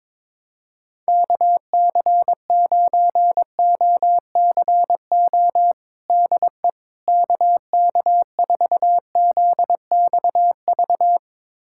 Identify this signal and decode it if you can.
Morse code, heard as a keyed tone, about 22 wpm, 700 Hz. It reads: KC9OCO DE KK4ZXV
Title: USA Callsign DE Callsign 22wpm